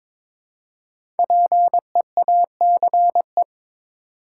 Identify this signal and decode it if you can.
Morse code, heard as a keyed tone, about 23 wpm, 700 Hz. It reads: PEACE
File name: 5 Letter Words - First 3 Letters Given - 22wpm